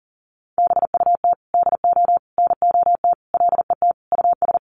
Morse code, heard as a keyed tone, about 40 wpm, 700 Hz. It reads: BUT DO NOT LET US
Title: Sentences from Top 1100 Words 40wpm